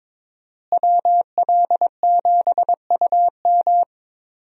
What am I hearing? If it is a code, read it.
Morse code, heard as a keyed tone, about 22 wpm, 700 Hz. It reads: WL7UM
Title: USA Call Signs - 5 Characters - Repeated 3x - 22wpm